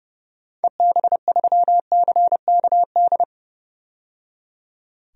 Morse code, heard as a keyed tone, about 30 wpm, 700 Hz. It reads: EB3CKD
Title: DXCC-100 Call Signs - 6 Characters - 30wpm